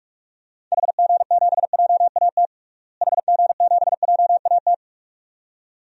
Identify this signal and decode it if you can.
Morse code, heard as a keyed tone, about 46 wpm, 700 Hz. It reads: HG7JAT HG7JAT
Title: DXCC-100 Call Signs - 6 Characters - Repeated 2x - 45wpm